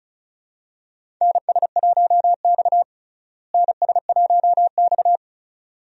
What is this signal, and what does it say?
Morse code, heard as a keyed tone, about 36 wpm, 700 Hz. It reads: NS1X NS1X
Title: USA Call Signs - 4 Characters - Repeated 2x - 35wpm